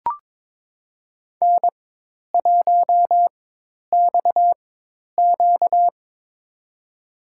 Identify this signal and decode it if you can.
Morse code, heard as a keyed tone, about 23 wpm, spaced as for 10 wpm, 700 Hz. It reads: N1XQ
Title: USA Call Signs - 4 Characters - 4x Character Spacing - 22wpm